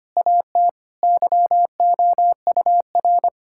Morse code, heard as a keyed tone, about 25 wpm, 700 Hz. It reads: AT YOUR
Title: Sentences from Top 1600 Words 25wpm